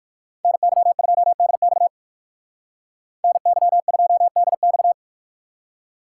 Morse code, heard as a keyed tone, about 56 wpm, 690 Hz. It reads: NY2DX NY2DX